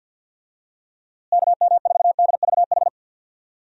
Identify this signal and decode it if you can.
Morse code, heard as a keyed tone, about 52 wpm, 700 Hz. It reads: KM4DVH
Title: USA Call Signs - 6 Characters - Repeated 3x - 50wpm